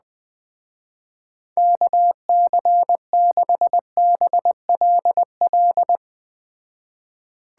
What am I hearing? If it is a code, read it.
Morse code, heard as a keyed tone, about 20 wpm, 700 Hz. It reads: KC6BLL
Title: Speed Racing - USA Call Signs 17wpm